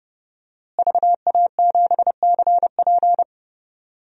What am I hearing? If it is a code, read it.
Morse code, heard as a keyed tone, about 30 wpm, 700 Hz. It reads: VA7CP
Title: DXCC-100 Call Signs - 5 Characters - Repeated 3x - 30wpm